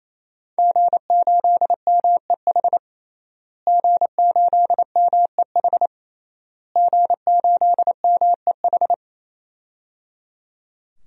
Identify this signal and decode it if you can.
Morse code, heard as a keyed tone, about 27 wpm, 700 Hz. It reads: G8MEH G8MEH G8MEH
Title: DXCC-100 Call Signs - 5 Characters - Repeated 3x - 28wpm